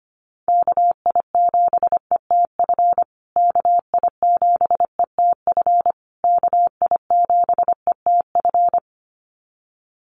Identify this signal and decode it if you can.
Morse code, heard as a keyed tone, about 25 wpm, 700 Hz. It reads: KI7ETF KI7ETF KI7ETF
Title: USA Call Signs Repeated 3x 25wpm